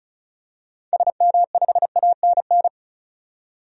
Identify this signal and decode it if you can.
Morse code, heard as a keyed tone, about 36 wpm, 700 Hz. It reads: SM5ANN
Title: DXCC-100 Call Signs - 6 Characters - 35wpm